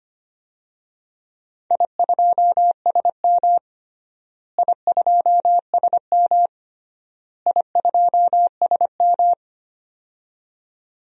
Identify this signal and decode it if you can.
Morse code, heard as a keyed tone, about 25 wpm, 700 Hz. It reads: I2SM I2SM I2SM
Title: DXCC-100 Call Signs - 4 Characters - Repeated 3x - 25wpm